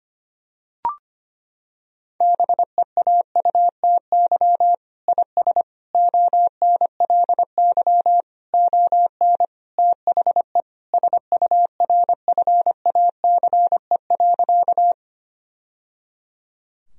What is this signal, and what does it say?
Morse code, heard as a keyed tone, about 25 wpm, 700 Hz. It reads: BEAUTY IS ONLY ON THE SURFACE.
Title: Sentences from Top 1000 Words - Review Part 3 - 25wpm